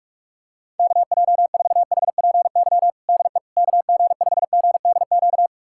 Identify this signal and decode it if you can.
Morse code, heard as a keyed tone, about 43 wpm, 700 Hz. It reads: KJ4HPY DE KG5GDQ
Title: USA Callsign DE Callsign 45wpm